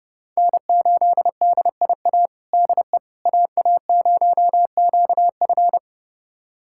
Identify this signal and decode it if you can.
Morse code, heard as a keyed tone, about 30 wpm, 700 Hz. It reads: N8DIA DE AA0QF
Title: USA Callsign DE Callsign - 5 Characters - 30wpm